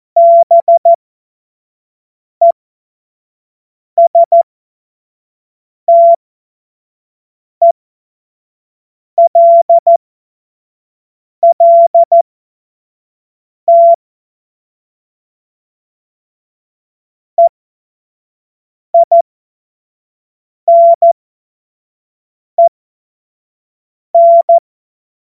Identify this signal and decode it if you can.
Morse code, heard as a keyed tone, about 14 wpm, spaced as for 5 wpm, 680 Hz. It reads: BESTELLT EINEN
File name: Morsekurs / Witze / Witz 12. Oktober 2021, 25 F70